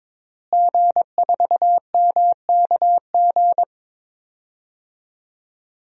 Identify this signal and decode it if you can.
Morse code, heard as a keyed tone, about 22 wpm, 700 Hz. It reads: G4MKG
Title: DXCC-100 Call Signs - 5 Characters - Repeated 3x - 22wpm